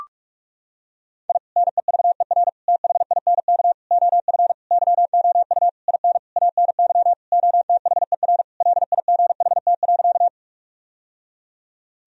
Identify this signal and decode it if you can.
Morse code, heard as a keyed tone, about 42 wpm, 700 Hz. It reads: I NEVER THINK OF YOU IN ANY OTHER LIGHT.
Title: Sentences from Top 1000 Words - Review Part 1 - 45wpm